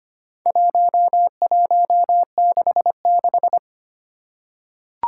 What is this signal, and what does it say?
Morse code, heard as a keyed tone, about 25 wpm, 700 Hz. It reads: 1166
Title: Four Challenging Letter-Number Combinations 25wpm